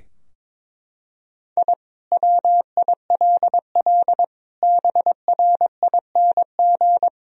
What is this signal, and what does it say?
Morse code, heard as a keyed tone, about 22 wpm, 700 Hz. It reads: I WILL BRING
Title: Sentences from Top 1000 Words - Review Part 3 - 22wpm